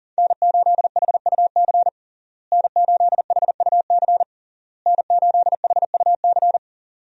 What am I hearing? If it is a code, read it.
Morse code, heard as a keyed tone, about 37 wpm, 700 Hz. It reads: N8HUC N8HUC N8HUC
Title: USA Call Signs - 5 Characters - Repeated 3x - 40wpm